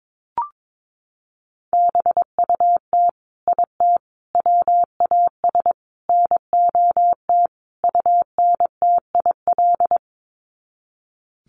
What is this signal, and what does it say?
Morse code, heard as a keyed tone, about 22 wpm, 700 Hz. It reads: BUT IT WAS NOT UNTIL
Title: Sets of 5 Words Encore2 22wpm